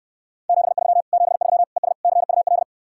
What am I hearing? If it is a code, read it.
Morse code, heard as a keyed tone, about 68 wpm, 700 Hz. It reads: 73 73 ES CUL